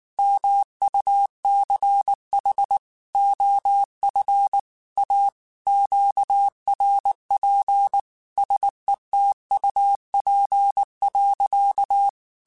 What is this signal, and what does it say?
Morse code, heard as a keyed tone, about 19 wpm, 800 Hz. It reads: MUCH OF A QRP SETUP.
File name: QSOs / QSOs-mp3s / WD0BGP 1@18